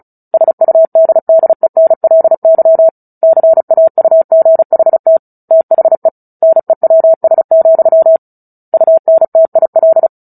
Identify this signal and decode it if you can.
Morse code, heard as a keyed tone, about 34 wpm, 650 Hz. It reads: SUDDENLY CAUGHT THE NEWS, UNTIL
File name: Book Two / 09-Chapter 9 - 35 WPM